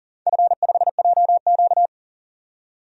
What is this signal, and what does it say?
Morse code, heard as a keyed tone, about 41 wpm, 700 Hz. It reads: F5JQ
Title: DXCC-100 Call Signs - 4 Characters - 40wpm